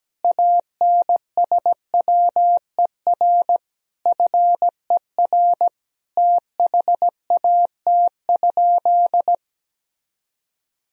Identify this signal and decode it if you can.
Morse code, heard as a keyed tone, about 17 wpm, 700 Hz. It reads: ANSWER FER THAT?
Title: Sentences from Top 300 Words and Abbreviations 17wpm